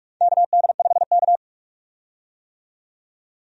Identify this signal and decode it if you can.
Morse code, heard as a keyed tone, about 46 wpm, 700 Hz. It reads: KD5K
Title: USA Call Signs - 4 Characters - 45wpm